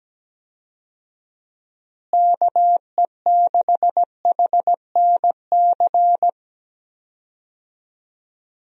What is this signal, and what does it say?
Morse code, heard as a keyed tone, about 17 wpm, 700 Hz. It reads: KE6HNC
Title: USA Call Signs - 6 Characters - Repeated 2x - 17wpm